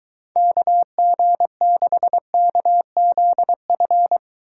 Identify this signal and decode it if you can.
Morse code, heard as a keyed tone, about 23 wpm, 700 Hz. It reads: KG6KZF